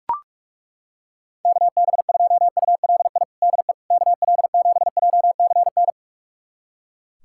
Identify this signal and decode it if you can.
Morse code, heard as a keyed tone, about 41 wpm, 700 Hz. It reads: KB2ULI DE KL7JCN
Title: USA Callsign DE Callsign 45wpm